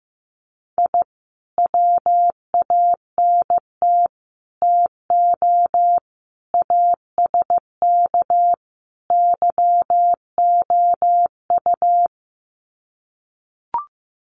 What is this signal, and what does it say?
Morse code, heard as a keyed tone, about 15 wpm, 700 Hz. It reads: I WANT TO ASK YOU
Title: Sets of 5 Words 15wpm